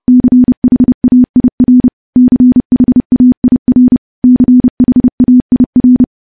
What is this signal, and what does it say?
Morse code, heard as a keyed tone, about 28 wpm, 250 Hz. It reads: CHAIR CHAIR CHAIR
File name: morse / morse-training-30WPM